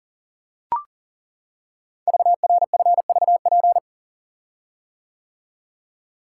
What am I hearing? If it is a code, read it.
Morse code, heard as a keyed tone, about 40 wpm, 700 Hz. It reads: VRFVP
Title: Code Groups - 5 Letters - 40wpm